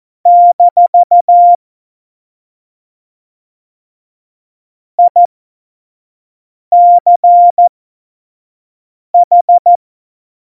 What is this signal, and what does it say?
Morse code, heard as a keyed tone, about 14 wpm, spaced as for 5 wpm, 700 Hz. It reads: - ICH
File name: Morsekurs / Witze / Witz 07. Januar 2020, 25 F70